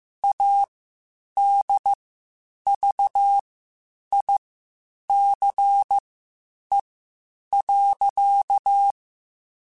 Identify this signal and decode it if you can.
Morse code, heard as a keyed tone, about 15 wpm, spaced as for 8 wpm, 800 Hz. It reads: ADVICE.
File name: Quotes Spaced / Quotes Spaced-mp3s / Advice sp @13